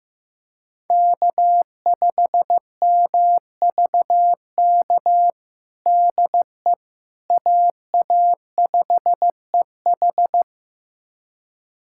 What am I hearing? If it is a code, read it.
Morse code, heard as a keyed tone, about 15 wpm, 700 Hz. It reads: K5MVK DE AA5EH
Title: USA Callsign DE Callsign - 5 Characters - 15wpm